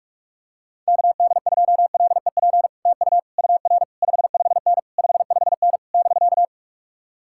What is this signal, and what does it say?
Morse code, heard as a keyed tone, about 49 wpm, 700 Hz. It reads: KD2LEP TU UR 55N 55N <BK>